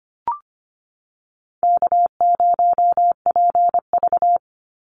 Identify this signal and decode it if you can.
Morse code, heard as a keyed tone, about 25 wpm, 700 Hz. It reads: K0PV
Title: Call Signs - 25wpm